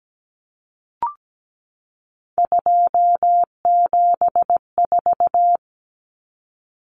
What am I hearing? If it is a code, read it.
Morse code, heard as a keyed tone, about 17 wpm, 700 Hz. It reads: 274
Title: Numbers - 3-Digit 17wpm